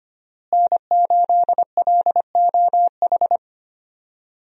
Morse code, heard as a keyed tone, about 25 wpm, 690 Hz. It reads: N8LOH
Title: USA Call Signs - 5 Characters - Repeated 2x - 25wpm